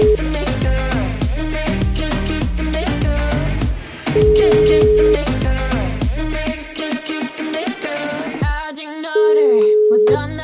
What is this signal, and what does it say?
Ringback tone (Europe, 425 Hz), with noise.